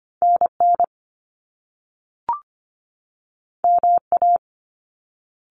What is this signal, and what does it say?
Morse code, heard as a keyed tone, about 25 wpm, 700 Hz. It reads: N N M A
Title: Top Two Letter Combinations in Words 25wpm